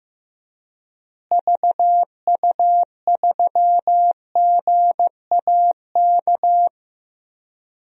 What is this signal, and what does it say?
Morse code, heard as a keyed tone, about 15 wpm, 700 Hz. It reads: VU3GAK
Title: DXCC-100 Call Signs - 6 Characters - Repeated 2x - 15wpm